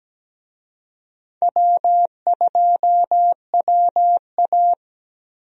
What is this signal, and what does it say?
Morse code, heard as a keyed tone, about 17 wpm, 700 Hz. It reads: W2WA
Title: USA Call Signs - 4 Characters - 17wpm